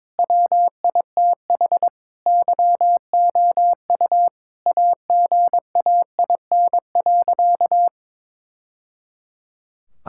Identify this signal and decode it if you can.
Morse code, heard as a keyed tone, about 22 wpm, 700 Hz. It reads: WITH YOU AGAIN.
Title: Sentences from Top 1000 Words - Review Part 6 - 22wpm